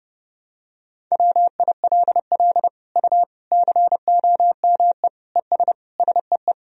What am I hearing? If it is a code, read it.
Morse code, heard as a keyed tone, about 30 wpm, 700 Hz. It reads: WILL U COME ES SEE
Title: Sentences from Top 100 Words and Abbreviations 30wpm